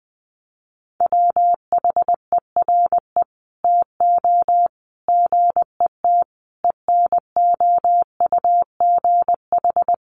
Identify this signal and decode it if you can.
Morse code, heard as a keyed tone, about 20 wpm, 700 Hz. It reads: WHERE TO GET ENOUGH
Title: Sentences from Top 1400 Words 20wpm